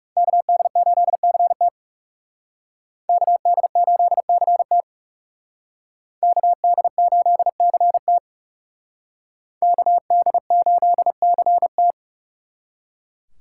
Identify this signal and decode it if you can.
Morse code, heard as a keyed tone, about 39 wpm, 700 Hz. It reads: KD8CT KD8CT KD8CT KD8CT